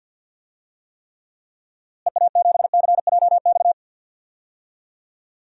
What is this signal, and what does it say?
Morse code, heard as a keyed tone, about 50 wpm, 700 Hz. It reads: EA7CJX
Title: DXCC-100 Call Signs - 6 Characters - Repeated 3x - 50wpm